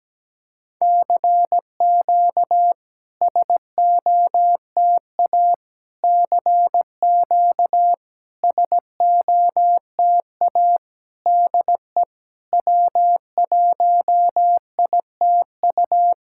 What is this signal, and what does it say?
Morse code, heard as a keyed tone, about 17 wpm, 700 Hz. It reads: CQ SOTA CQ SOTA DE W1ITU
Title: SOTA QSO Practice 17wpm